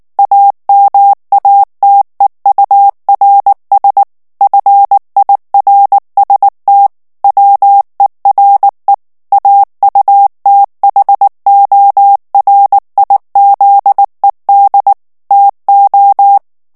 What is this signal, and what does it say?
Morse code, heard as a keyed tone, about 19 wpm, 800 Hz. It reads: AMATEURS FIRST WERE AUTHORIZED TO
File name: ANGLAIS texte clair 3 vitesse 18 mots minute